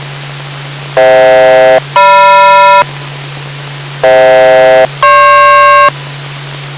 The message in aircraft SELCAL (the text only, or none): FH-GM FH-GP